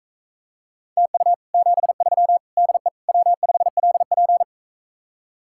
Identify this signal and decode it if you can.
Morse code, heard as a keyed tone, about 44 wpm, 700 Hz. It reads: TU 73 DE W5LP